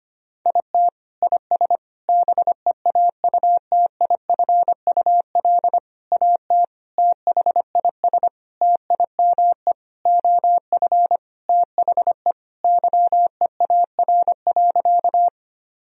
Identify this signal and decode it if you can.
Morse code, heard as a keyed tone, about 25 wpm, 700 Hz. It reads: IT IS BEAUTIFUL AT THIS TIME OF THE YEAR.
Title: Sentences from Top 1000 Words - Review Part 1 - 25wpm